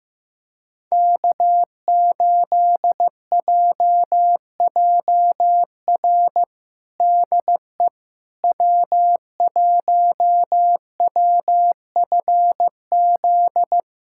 Morse code, heard as a keyed tone, about 15 wpm, 700 Hz. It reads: K8JJR DE W1WFZ
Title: USA Callsign DE Callsign - 5 Characters - 15wpm